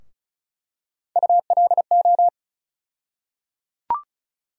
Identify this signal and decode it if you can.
Morse code, heard as a keyed tone, about 34 wpm, 700 Hz. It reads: ULO
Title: USA Call Sign Postfixes - 3 Letters 35wpm